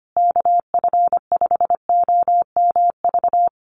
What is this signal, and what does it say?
Morse code, heard as a keyed tone, about 25 wpm, 700 Hz. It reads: KF5OMV
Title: USA Call Signs Repeated 2x 25wpm